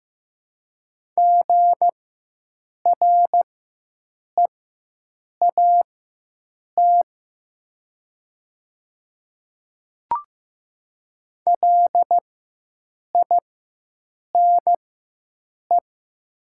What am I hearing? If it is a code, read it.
Morse code, heard as a keyed tone, about 15 wpm, spaced as for 7 wpm, 700 Hz. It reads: GREAT LINE
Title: 4x Character Spacing - Top 200 Words - 15wpm